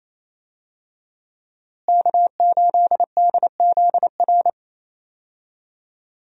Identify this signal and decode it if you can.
Morse code, heard as a keyed tone, about 28 wpm, 700 Hz. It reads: K8DZR
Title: USA Call Signs - 5 Characters - Repeated 2x - 28wpm